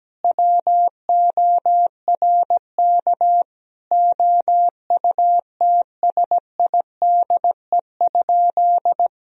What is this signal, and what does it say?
Morse code, heard as a keyed tone, about 17 wpm, 700 Hz. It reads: WORK OUTSIDE?
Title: Sentences from Top 1000 Words - Review Part 4 - 17wpm